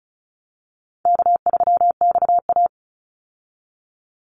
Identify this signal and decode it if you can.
Morse code, heard as a keyed tone, about 34 wpm, 700 Hz. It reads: K3XA
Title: Easy Call Signs Repeated 3x - 35wpm